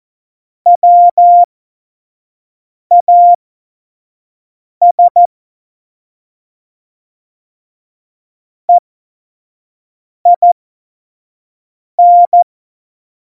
Morse code, heard as a keyed tone, about 14 wpm, spaced as for 5 wpm, 700 Hz. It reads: WAS EIN
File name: Morsekurs / Witze / Witz 23. Juli 2019, 25 F70